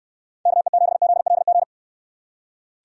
Morse code, heard as a keyed tone, about 69 wpm, 700 Hz. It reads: KE7ZLG